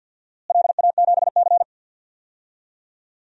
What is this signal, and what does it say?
Morse code, heard as a keyed tone, about 50 wpm, 700 Hz. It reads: LA7C
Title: DXCC-100 Call Signs - 4 Characters - Repeated 3x - 50wpm